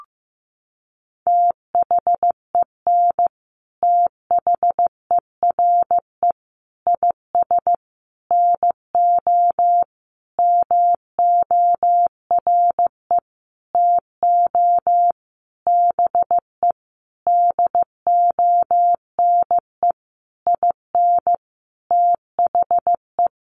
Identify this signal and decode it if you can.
Morse code, heard as a keyed tone, about 15 wpm, 700 Hz. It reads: THEN THERE IS NO MORE TO BE DONE IN THE
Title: Sentences from Top 700 Words 15wpm